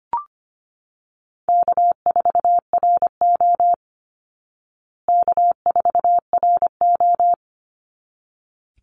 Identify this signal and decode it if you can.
Morse code, heard as a keyed tone, about 25 wpm, 700 Hz. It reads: K4RO K4RO
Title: Call Signs Repeated 3x - 25wpm